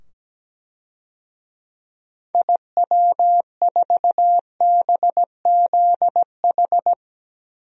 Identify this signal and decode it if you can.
Morse code, heard as a keyed tone, about 17 wpm, 700 Hz. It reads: IW4BZH
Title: DXCC-100 Call Signs - 6 Characters - 17wpm